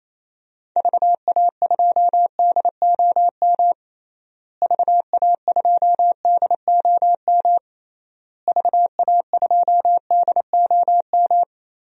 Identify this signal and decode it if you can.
Morse code, heard as a keyed tone, about 27 wpm, 700 Hz. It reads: VA2DOM VA2DOM VA2DOM
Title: DXCC-100 Call Signs - 6 Characters - Repeated 3x - 28wpm